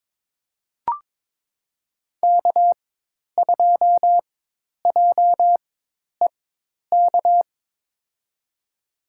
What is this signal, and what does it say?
Morse code, heard as a keyed tone, about 22 wpm, spaced as for 10 wpm, 700 Hz. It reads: K2JEK
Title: USA Call Signs - 5 Characters - 4x Character Spacing - 22wpm